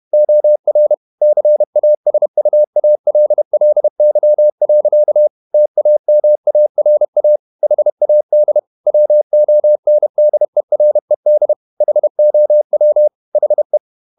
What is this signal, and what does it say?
Morse code, heard as a keyed tone, about 31 wpm, 600 Hz. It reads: OR CASUALLY. TAMARA HAD WONDERED HOW HE
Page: Never 31wpm | CW med Gnister